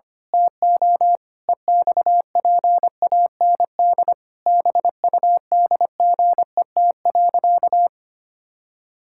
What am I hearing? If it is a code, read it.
Morse code, heard as a keyed tone, about 25 wpm, 700 Hz. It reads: TO EXPAND BUDGET.
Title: Pangram Challenge v2 - 25wpm